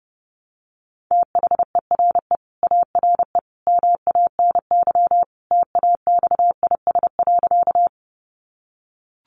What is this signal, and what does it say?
Morse code, heard as a keyed tone, about 30 wpm, 700 Hz. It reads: THERE ARE MANY TAXIS.
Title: Easy Phrases 30wpm